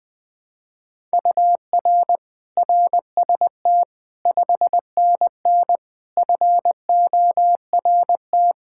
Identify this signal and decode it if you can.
Morse code, heard as a keyed tone, about 20 wpm, 700 Hz. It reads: UR RST 5NN FORT
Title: Practice QSO - Exchange 1 - Beginning - 20wpm